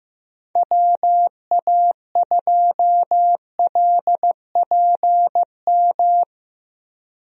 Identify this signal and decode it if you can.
Morse code, heard as a keyed tone, about 15 wpm, 700 Hz. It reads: WA2LPM
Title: USA Call Signs - 6 Characters - Repeated 3x - 15wpm